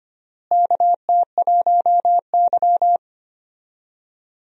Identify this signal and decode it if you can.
Morse code, heard as a keyed tone, about 25 wpm, 700 Hz. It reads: KT1Y
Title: USA Call Signs - 4 Characters - Repeated 2x - 25wpm